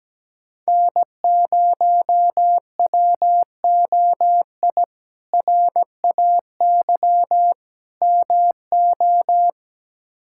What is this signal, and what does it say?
Morse code, heard as a keyed tone, about 17 wpm, 700 Hz. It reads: N0WOI RAY MO
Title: K1USN SST QSO Practice - 17wpm